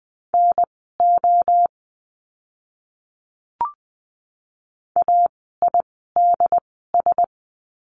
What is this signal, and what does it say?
Morse code, heard as a keyed tone, about 20 wpm, spaced as for 14 wpm, 700 Hz. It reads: NO AIDS
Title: 068 Lesson 12f - Letter d - Words 20wpm